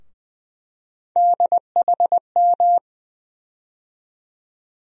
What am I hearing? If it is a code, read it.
Morse code, heard as a keyed tone, about 20 wpm, 700 Hz. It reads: DHM
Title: USA Call Sign Postfixes - 3 Letters 20wpm